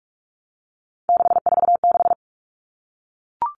Morse code, heard as a keyed tone, about 44 wpm, 700 Hz. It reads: 646
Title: Numbers - 3-Digit 45wpm